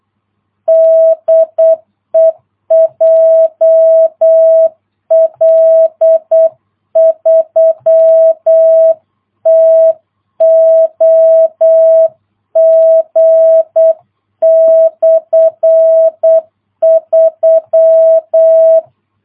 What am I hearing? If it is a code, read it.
Morse code, heard as a keyed tone, about 8 wpm, 660 Hz. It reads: DE JL3TOG/3
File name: jh8cbh / 2022cw